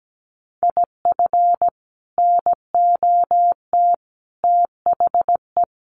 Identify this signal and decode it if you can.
Morse code, heard as a keyed tone, about 17 wpm, 700 Hz. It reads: IF NOT THE
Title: Sentences from Top 1400 Words 17wpm